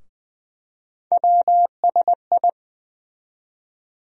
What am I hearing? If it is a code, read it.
Morse code, heard as a keyed tone, about 20 wpm, 700 Hz. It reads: WSI
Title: USA Call Sign Postfixes - 3 Letters 20wpm